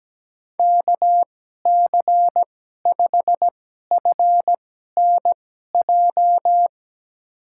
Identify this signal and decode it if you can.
Morse code, heard as a keyed tone, about 17 wpm, spaced as for 12 wpm, 700 Hz. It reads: KC5FNJ
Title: USA Call Signs - 2x Character Spacing - 17wpm